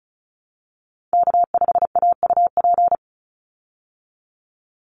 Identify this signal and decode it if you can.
Morse code, heard as a keyed tone, about 34 wpm, 700 Hz. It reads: K5AUP
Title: Call Signs Repeated 3x - 35wpm